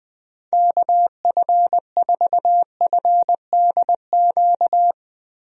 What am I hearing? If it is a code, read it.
Morse code, heard as a keyed tone, about 20 wpm, 700 Hz. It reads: KF4FDQ
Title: USA Call Signs - 6 Characters - Repeated 3x - 20wpm